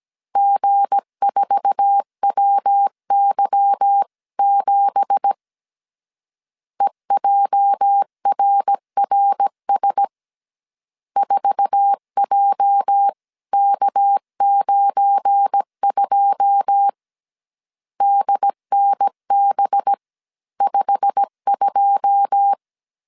Morse code, heard as a keyed tone, about 17 wpm, 800 Hz. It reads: G4WY7 EJRRS 4JK92 DNB52
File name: CW / DET / 1 9 1500C S C 16wpm 2909